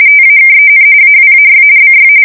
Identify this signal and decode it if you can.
RTTY, about 45.45 baud. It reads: TAKEN AWAY BY
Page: 170 Shift 2125/2295